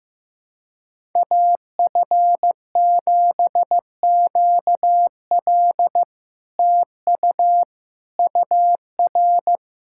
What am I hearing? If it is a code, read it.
Morse code, heard as a keyed tone, about 15 wpm, 700 Hz. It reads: AF7QL TU UR
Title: SOTA QSO Practice 15wpm